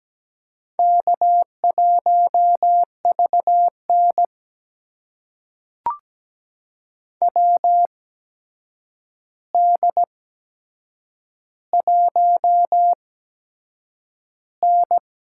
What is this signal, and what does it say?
Morse code, heard as a keyed tone, about 17 wpm, spaced as for 7 wpm, 700 Hz. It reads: K1VN WD1N
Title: USA Call Signs - 4 Characters - 8x Character Spacing - 17wpm